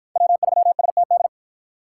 Morse code, heard as a keyed tone, about 53 wpm, 700 Hz. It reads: W3STD